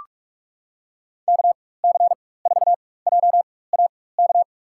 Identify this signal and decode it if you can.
Morse code, heard as a keyed tone, about 43 wpm, spaced as for 21 wpm, 700 Hz. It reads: KC4JAK
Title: USA Call Signs - 4x Character Spacing - 45wpm